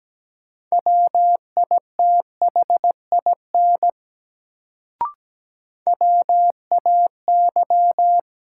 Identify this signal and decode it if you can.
Morse code, heard as a keyed tone, about 17 wpm, 700 Hz. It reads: WITHIN WAY
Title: Sets of 2 Words - New Word Drill - 17wpm